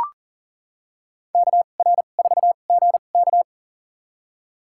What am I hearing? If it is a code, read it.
Morse code, heard as a keyed tone, about 39 wpm, spaced as for 29 wpm, 700 Hz. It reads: KR4GK
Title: USA Call Signs - 2x Character Spacing - 40wpm